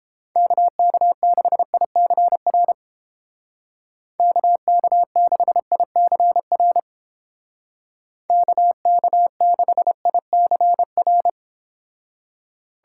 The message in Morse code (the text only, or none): KK6ICR KK6ICR KK6ICR